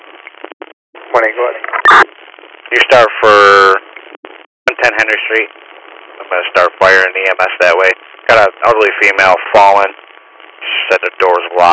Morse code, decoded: none

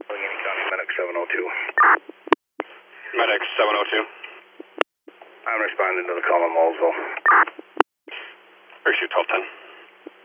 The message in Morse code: none

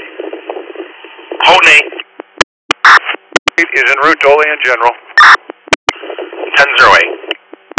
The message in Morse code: none

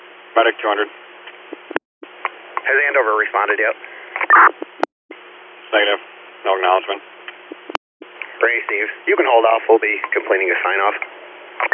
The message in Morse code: none

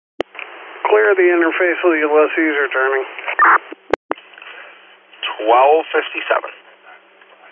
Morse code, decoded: none

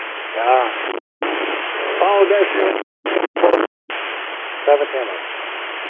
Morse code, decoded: none

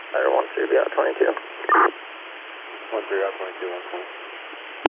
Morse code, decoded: none

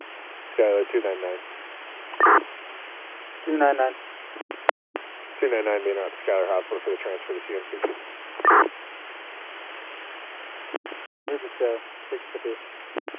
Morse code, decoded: none

